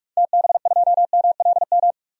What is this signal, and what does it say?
Morse code, heard as a keyed tone, about 46 wpm, 700 Hz. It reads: TB2MLM
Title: DXCC-100 Call Signs - 6 Characters - 45wpm